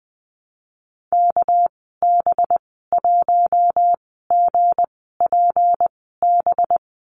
Morse code, heard as a keyed tone, about 20 wpm, spaced as for 14 wpm, 700 Hz. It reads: KB1GPB
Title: 200 Lesson 32g - Letter k - Call Signs 20wpm